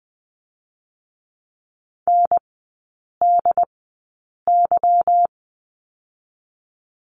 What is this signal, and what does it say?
Morse code, heard as a keyed tone, about 20 wpm, 690 Hz. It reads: N D Y
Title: 145 Lesson 24e - Letter y - Three Characters 20wpm